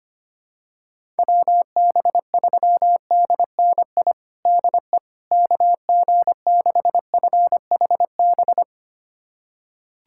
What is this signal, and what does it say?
Morse code, heard as a keyed tone, about 25 wpm, 700 Hz. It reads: WB3DNI DE KG6FHB
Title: USA Callsign DE Callsign - 6 Characters - 25wpm